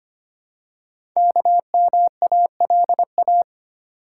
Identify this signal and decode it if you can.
Morse code, heard as a keyed tone, about 25 wpm, 700 Hz. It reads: KMALA
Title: Code Groups - 5 Letters - 25wpm